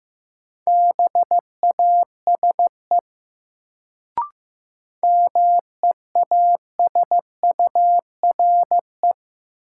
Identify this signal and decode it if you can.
Morse code, heard as a keyed tone, about 15 wpm, 700 Hz. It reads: BASE MEASURE
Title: Sentences from Top 1000 Words - Review Part 1 - New Word Drill - 15wpm